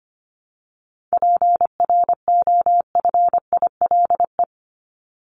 Morse code, heard as a keyed tone, about 25 wpm, 700 Hz. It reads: PROFILE
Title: Top 1501-1750 Words 25wpm